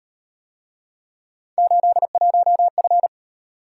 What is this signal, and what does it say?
Morse code, heard as a keyed tone, about 37 wpm, 700 Hz. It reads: 81F